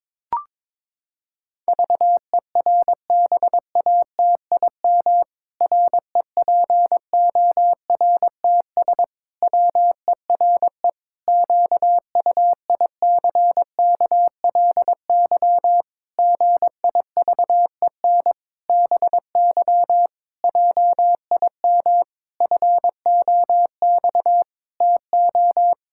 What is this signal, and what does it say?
Morse code, heard as a keyed tone, about 22 wpm, 700 Hz. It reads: VERBATIM REPORTS WERE QUICKLY GIVEN BY JIM FOX TO
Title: Pangram Challenge v2 - 22wpm